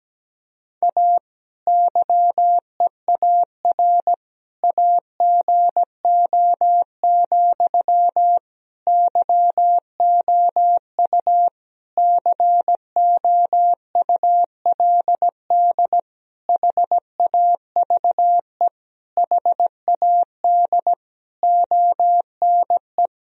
Morse code, heard as a keyed tone, about 17 wpm, 700 Hz. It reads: A YEAR AGO, YOU COULD HAVE HAD ONE
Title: Sentences from Top 1000 Words - Review Part 3 - 17wpm